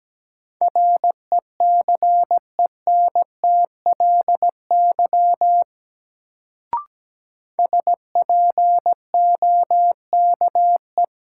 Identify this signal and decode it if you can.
Morse code, heard as a keyed tone, about 17 wpm, 700 Hz. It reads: RECENTLY SPOKE
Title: Sentences from Top 1200 Words - New Word Drill Alt - 17wpm